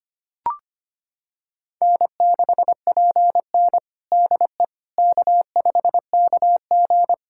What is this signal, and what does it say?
Morse code, heard as a keyed tone, about 25 wpm, 700 Hz. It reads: N6PN DE K5KG
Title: Practice QSO - Exchange 4 - Ending - 25wpm